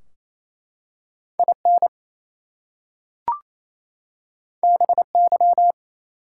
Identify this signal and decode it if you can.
Morse code, heard as a keyed tone, about 27 wpm, 700 Hz. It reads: IN BY
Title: Two Letter Words - 28wpm